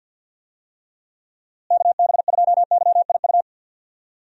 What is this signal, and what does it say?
Morse code, heard as a keyed tone, about 51 wpm, 700 Hz. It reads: KB2YIU
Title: USA Call Signs - 6 Characters - Repeated 3x - 50wpm